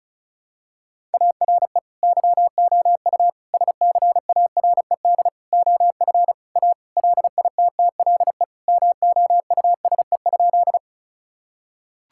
Morse code, heard as a keyed tone, about 36 wpm, 700 Hz. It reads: ARE YOU SCARED OF A LITTLE MOUSE?
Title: Sentences from Top 1000 Words - Review Part 5 - 35wpm